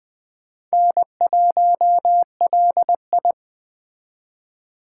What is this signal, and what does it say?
Morse code, heard as a keyed tone, about 20 wpm, 700 Hz. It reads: N1LI
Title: USA Call Signs - 4 Characters - 3x Character Spacing - 20wpm